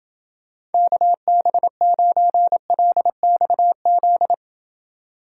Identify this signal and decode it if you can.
Morse code, heard as a keyed tone, about 27 wpm, 700 Hz. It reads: KB9LXZ